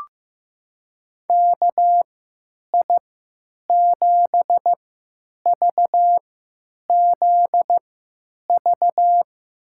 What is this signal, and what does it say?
Morse code, heard as a keyed tone, about 15 wpm, spaced as for 9 wpm, 700 Hz. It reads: KI7VZV
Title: USA Call Signs - 6 Characters - 3x Character Spacing - 15wpm